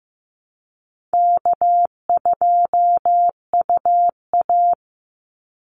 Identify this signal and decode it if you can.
Morse code, heard as a keyed tone, about 15 wpm, 700 Hz. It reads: K2UA
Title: Call Signs Repeated 2x - 15wpm